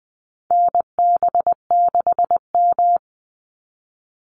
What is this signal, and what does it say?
Morse code, heard as a keyed tone, about 20 wpm, 700 Hz. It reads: NB6M
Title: Call Signs Repeated 2x - 20wpm